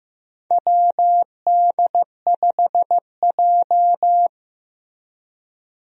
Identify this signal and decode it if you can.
Morse code, heard as a keyed tone, about 15 wpm, 700 Hz. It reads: WD5J
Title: USA Call Signs - 4 Characters - Repeated 2x - 15wpm